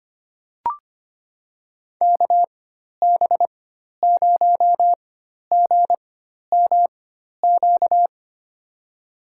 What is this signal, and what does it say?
Morse code, heard as a keyed tone, about 25 wpm, spaced as for 12 wpm, 700 Hz. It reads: KB0GMQ
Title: USA Call Signs - 6 Characters - 4x Character Spacing - 25wpm